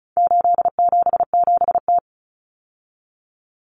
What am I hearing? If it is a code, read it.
Morse code, heard as a keyed tone, about 34 wpm, 700 Hz. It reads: 877T
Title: Common Cut Numbers - 4-Digit 35wpm